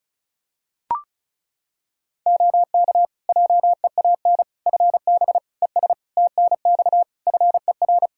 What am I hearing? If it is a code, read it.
Morse code, heard as a keyed tone, about 34 wpm, 700 Hz. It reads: OK JEAN FB ES TNX FER
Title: Practice QSO - Exchange 2 - Info - 35wpm